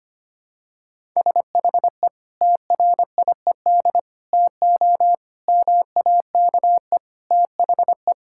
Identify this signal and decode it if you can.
Morse code, heard as a keyed tone, about 25 wpm, 700 Hz. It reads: SHE TRIED TO MAKE THE
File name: practice / 25wpm / Sentences with 3-Word Phrases - 25wpm